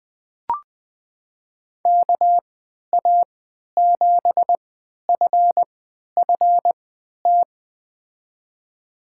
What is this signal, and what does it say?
Morse code, heard as a keyed tone, about 20 wpm, spaced as for 11 wpm, 700 Hz. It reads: KA7FFT
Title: USA Call Signs - 6 Characters - 3x Character Spacing - 20wpm